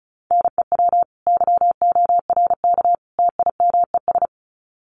Morse code, heard as a keyed tone, about 34 wpm, 700 Hz. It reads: NEW YORK TIMES
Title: Sets of 3 Words 35wpm